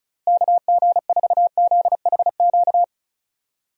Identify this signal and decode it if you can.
Morse code, heard as a keyed tone, about 36 wpm, 700 Hz. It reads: KG4ZHQ
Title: USA Call Signs - 6 Characters - Repeated 3x - 35wpm